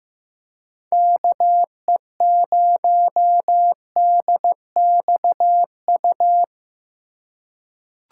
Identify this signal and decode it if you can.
Morse code, heard as a keyed tone, about 15 wpm, 700 Hz. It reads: KE0DXU
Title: USA Call Signs - 2x Character Spacing - 15wpm